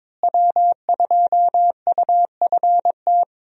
Morse code, heard as a keyed tone, about 22 wpm, 700 Hz. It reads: W2UFT
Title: USA Call Signs - 5 Characters - Repeated 3x - 22wpm